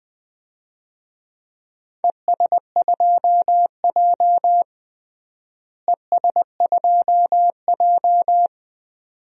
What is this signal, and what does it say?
Morse code, heard as a keyed tone, about 21 wpm, 700 Hz. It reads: ES2J ES2J
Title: DXCC-100 Call Signs - 4 Characters - Repeated 2x - 20wpm